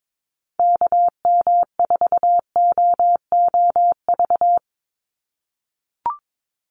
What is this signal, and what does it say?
Morse code, heard as a keyed tone, about 22 wpm, 700 Hz. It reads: KM4OOV
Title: USA Call Signs 22wpm